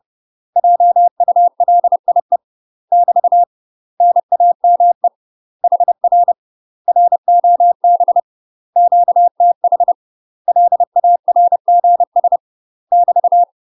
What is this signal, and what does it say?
Morse code, heard as a keyed tone, about 30 wpm, 700 Hz. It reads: JULIE <BT> NAME HR ROB QTH LARGS <BT>
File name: UK MASTER QSOs TRAINING / 30WPM / UK MASTER QSOs 0013